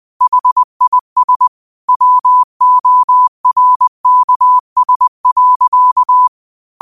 Morse code, heard as a keyed tone, about 20 wpm, 1000 Hz. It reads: HIS WORKS.
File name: qotdfiles / 20wpm